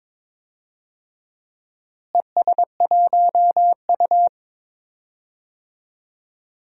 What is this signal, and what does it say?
Morse code, heard as a keyed tone, about 22 wpm, 700 Hz. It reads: ES1U
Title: DXCC-100 Call Signs - 4 Characters - Repeated 3x - 22wpm